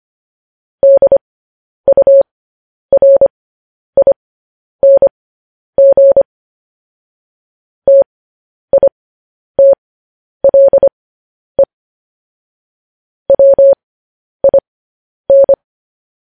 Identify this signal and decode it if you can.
Morse code, heard as a keyed tone, about 25 wpm, spaced as for 10 wpm, 550 Hz. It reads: DURING TITLE WIN
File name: podcast / News headlines-1x-25x10